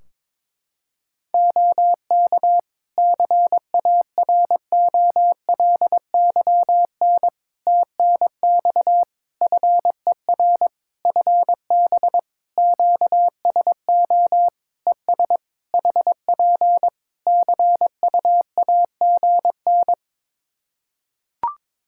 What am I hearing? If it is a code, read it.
Morse code, heard as a keyed tone, about 22 wpm, 700 Hz. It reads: OK CAROLYN TNX FER FB QSO ES HP CUAGN
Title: Practice QSO - Exchange 4 - Ending - 22wpm